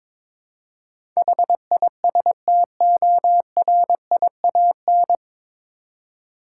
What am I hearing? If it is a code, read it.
Morse code, heard as a keyed tone, about 22 wpm, 700 Hz. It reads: HISTORIAN
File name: 9 Letter Words - First 3 Letters Given - 22wpm